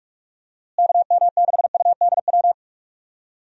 Morse code, heard as a keyed tone, about 44 wpm, 700 Hz. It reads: KM6UDW
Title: USA Call Signs - 6 Characters - Repeated 3x - 45wpm